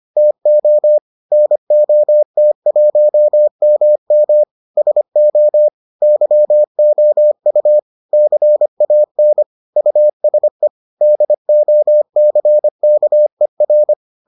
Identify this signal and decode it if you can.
Morse code, heard as a keyed tone, about 25 wpm, 600 Hz. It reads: TO NOT1MM SO YOU CAN USE DOCKER